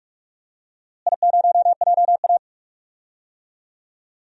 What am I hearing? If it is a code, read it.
Morse code, heard as a keyed tone, about 46 wpm, 700 Hz. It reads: I0JA
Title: DXCC-100 Call Signs - 4 Characters - Repeated 2x - 45wpm